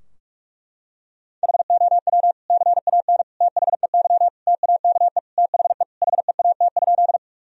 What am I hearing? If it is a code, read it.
Morse code, heard as a keyed tone, about 46 wpm, 700 Hz. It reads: HOW CAN THEY TAKE THE HEAT?
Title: Sentences from Top 1000 Words - Review Part 3 - 45wpm